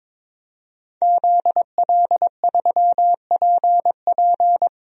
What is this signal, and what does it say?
Morse code, heard as a keyed tone, about 22 wpm, 700 Hz. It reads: ZL3PP
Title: DXCC-100 Call Signs - 5 Characters - Repeated 2x - 22wpm